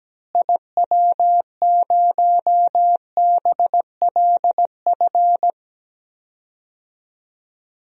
Morse code, heard as a keyed tone, about 17 wpm, 700 Hz. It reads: IW0BLF
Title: DXCC-100 Call Signs - 6 Characters - Repeated 3x - 17wpm